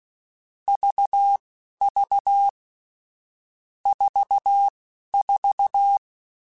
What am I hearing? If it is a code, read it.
Morse code, heard as a keyed tone, about 16 wpm, spaced as for 12 wpm, 800 Hz. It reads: VV 44
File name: Difficult Combos / Difficult Combos-mp3s / AUV AUV4 @10